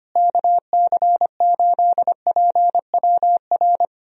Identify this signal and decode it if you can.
Morse code, heard as a keyed tone, about 25 wpm, 700 Hz. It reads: KC8PWR
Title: USA Call Signs - 6 Characters - Repeated 3x - 25wpm